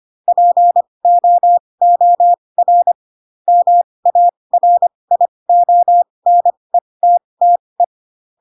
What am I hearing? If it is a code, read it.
Morse code, heard as a keyed tone, about 25 wpm, spaced as for 20 wpm, 700 Hz. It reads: POOR MARIONETTE